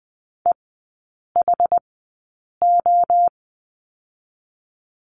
Morse code, heard as a keyed tone, about 19 wpm, spaced as for 8 wpm, 690 Hz. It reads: EHO
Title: 060 Lesson 11e - Letter h - Three Characters 20wpm